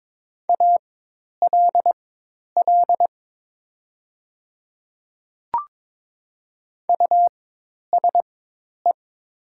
Morse code, heard as a keyed tone, about 22 wpm, spaced as for 10 wpm, 700 Hz. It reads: ALL USE
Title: 4x Character Spacing - Top 100 Words - 22wpm